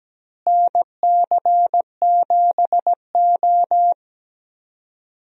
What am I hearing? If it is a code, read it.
Morse code, heard as a keyed tone, about 17 wpm, 700 Hz. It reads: NC7O
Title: USA Call Signs - 4 Characters - Repeated 3x - 17wpm